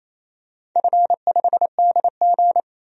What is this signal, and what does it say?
Morse code, heard as a keyed tone, about 28 wpm, 700 Hz. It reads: F5DG
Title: DXCC-100 Call Signs - 4 Characters - 28wpm